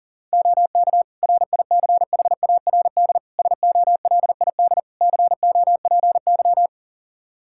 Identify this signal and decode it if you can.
Morse code, heard as a keyed tone, about 40 wpm, 700 Hz. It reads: OK RICHARD SOLID COPY
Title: Practice QSO - Exchange 3 - More Info - 40wpm